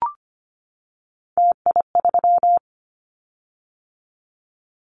Morse code, heard as a keyed tone, about 25 wpm, 700 Hz. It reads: TI3
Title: Callsign Prefixes for Top 100 DXCC - 25wpm